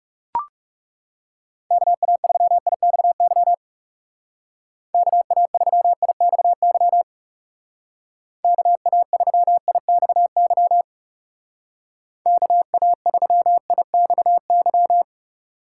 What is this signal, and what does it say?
Morse code, heard as a keyed tone, about 37 wpm, 700 Hz. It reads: KA3IXY KA3IXY KA3IXY KA3IXY
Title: Speed Racing - USA 2x3 Call Signs 30wpm